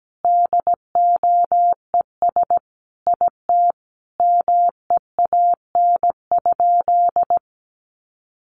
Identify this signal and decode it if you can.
Morse code, heard as a keyed tone, about 17 wpm, 700 Hz. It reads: DOES IT MEAN?
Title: Sentences from Top 200 Words 17wpm